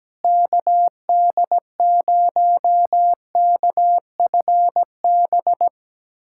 Morse code, heard as a keyed tone, about 17 wpm, 700 Hz. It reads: KD0KFB
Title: USA Callsign DE Callsign - 6 Characters - 17wpm